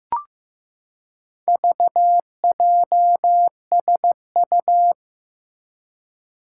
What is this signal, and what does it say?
Morse code, heard as a keyed tone, about 15 wpm, 700 Hz. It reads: VJSU
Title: Four Challenging Letter-Number Combinations 15wpm